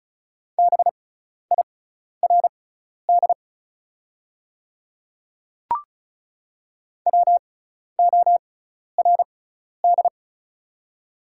Morse code, heard as a keyed tone, about 36 wpm, spaced as for 12 wpm, 700 Hz. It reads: BIRD WORD
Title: 6x Character Spacing - Top 500 Words - 35wpm